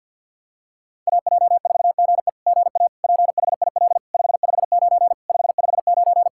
Morse code, heard as a keyed tone, about 47 wpm, 690 Hz. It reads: AJ4GE GA PHIL 559 559
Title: SOTA QSO Practice 50wpm